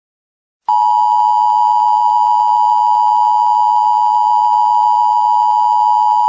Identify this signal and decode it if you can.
Emergency Alert System (SAME) tones (the 853 and 960 Hz attention tone). It is heard from 0.6 to 6.3 s.